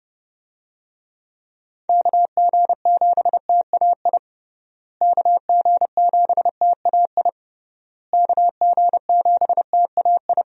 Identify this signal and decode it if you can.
Morse code, heard as a keyed tone, about 30 wpm, 700 Hz. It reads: KG7TAI KG7TAI KG7TAI
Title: USA Call Signs - 6 Characters - Repeated 3x - 30wpm